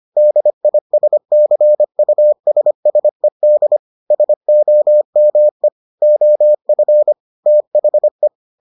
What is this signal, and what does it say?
Morse code, heard as a keyed tone, about 25 wpm, 600 Hz. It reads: DISCUSSED SOME OF THE